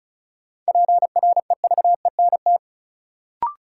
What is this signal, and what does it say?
Morse code, heard as a keyed tone, about 34 wpm, 700 Hz. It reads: PREVENT
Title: Sentences from Top 1100 Words - New Word Drill Alt - 35wpm